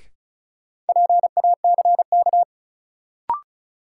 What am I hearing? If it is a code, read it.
Morse code, heard as a keyed tone, about 34 wpm, 700 Hz. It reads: PACK
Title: Sentences from Top 1300 Words - New Word Drill Alt - 35wpm